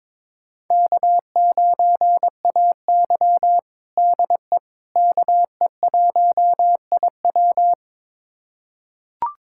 Morse code, heard as a keyed tone, about 21 wpm, 700 Hz. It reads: K9AY DE KE1IW
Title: USA Callsign DE Callsign 22wpm